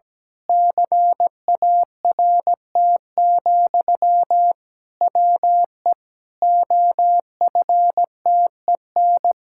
Morse code, heard as a keyed tone, about 17 wpm, 700 Hz. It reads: CART, WE OFTEN
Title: Pangram Challenge v2 - 17wpm